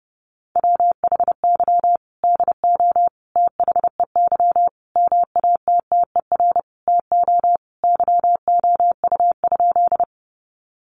Morse code, heard as a keyed tone, about 30 wpm, 700 Hz. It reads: WHY DO THEY MATTER TO YOU?
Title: Sentences from Top 700 Words 30wpm